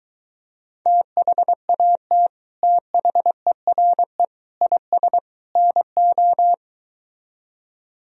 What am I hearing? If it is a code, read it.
Morse code, heard as a keyed tone, about 23 wpm, 700 Hz. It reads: THAT THERE IS NO